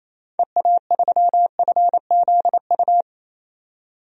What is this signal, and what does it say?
Morse code, heard as a keyed tone, about 28 wpm, 700 Hz. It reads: EA3FZU
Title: DXCC-100 Call Signs - 6 Characters - 28wpm